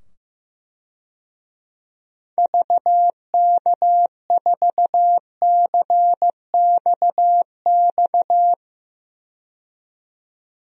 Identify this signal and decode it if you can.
Morse code, heard as a keyed tone, about 16 wpm, 700 Hz. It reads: VK4CXX
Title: DXCC-100 Call Signs - 6 Characters - 15wpm